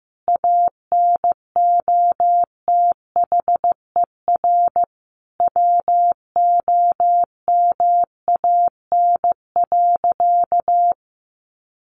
Morse code, heard as a keyed tone, about 15 wpm, 700 Hz. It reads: ANOTHER WOMAN.
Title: Sentences from Top 1400 Words 15wpm